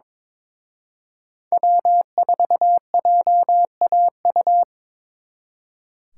Morse code, heard as a keyed tone, about 25 wpm, 700 Hz. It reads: W4JAU
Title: Speed Racing - USA Call Signs 22wpm